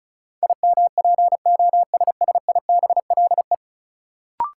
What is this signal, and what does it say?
Morse code, heard as a keyed tone, about 34 wpm, 700 Hz. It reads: IMPOSSIBLE
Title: Sentences from Top 1100 Words - New Word Drill - 35wpm